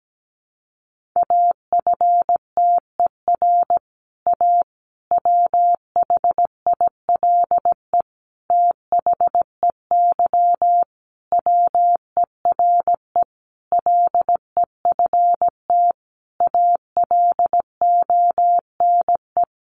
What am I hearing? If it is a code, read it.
Morse code, heard as a keyed tone, about 17 wpm, 700 Hz. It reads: AFTER A WHILE THEY WERE LEFT ALONE
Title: Sentences from Top 1000 Words 17wpm